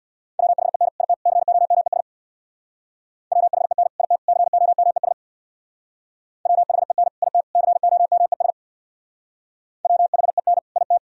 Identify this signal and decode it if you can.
Morse code, heard as a keyed tone, about 64 wpm, 700 Hz. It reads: WHEN IT COMES WHEN IT COMES WHEN IT COMES WHEN IT